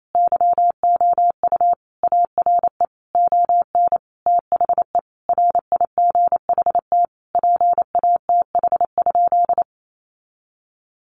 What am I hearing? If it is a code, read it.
Morse code, heard as a keyed tone, about 28 wpm, 700 Hz. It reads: YOU ARE ON THE RIGHT PATH?
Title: Sentences from Top 1400 Words 28wpm